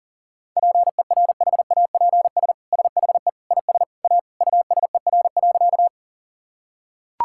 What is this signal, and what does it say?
Morse code, heard as a keyed tone, about 40 wpm, 690 Hz. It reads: PERHAPS SHE IS A USER.
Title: Sentences from Top 1000 Words - Review Part 5 - 40wpm